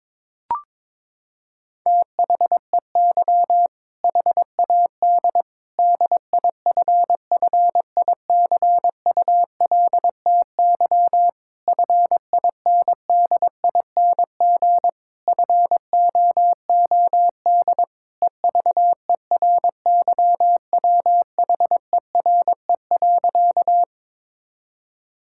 Morse code, heard as a keyed tone, about 22 wpm, 700 Hz. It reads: THEY HAD DIFFICULTY FINDING FOOD EVERYWHERE.
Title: Sentences from Top 1600 Words 22wpm